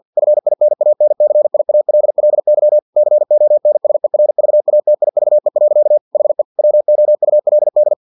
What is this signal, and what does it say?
Morse code, heard as a keyed tone, about 50 wpm, 600 Hz. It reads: FINANCIALLY CONSERVATIVE. HE WOULD
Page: Never 49wpm